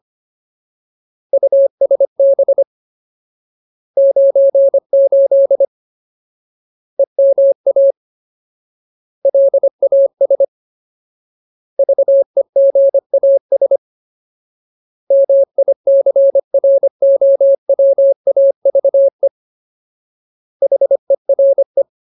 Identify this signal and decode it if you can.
Morse code, heard as a keyed tone, about 25 wpm, 550 Hz. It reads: USB 98 EMA LAS VEGAS MICROWAVE HERE
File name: Ultimate Word List / 25WPM Ultimate